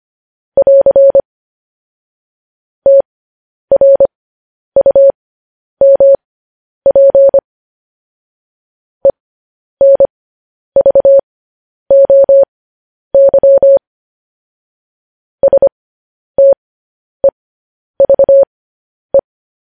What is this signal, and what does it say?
Morse code, heard as a keyed tone, about 25 wpm, spaced as for 10 wpm, 550 Hz. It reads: <AR> TRUMP ENVOY STEVE
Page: news-headlines-in-morse-code-at-10-wpm